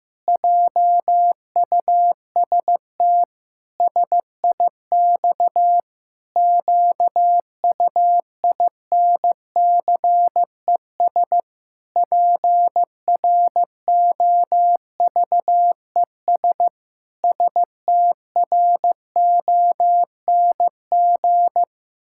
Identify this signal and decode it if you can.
Morse code, heard as a keyed tone, about 16 wpm, 700 Hz. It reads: JUST SIX QUINCES PROVES STRONG
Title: Pangram Challenge v2 - 15wpm